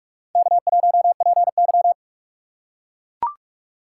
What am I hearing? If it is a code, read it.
Morse code, heard as a keyed tone, about 43 wpm, 700 Hz. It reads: K1PY
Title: USA Call Signs - 4 Characters - 6x Character Spacing - 45wpm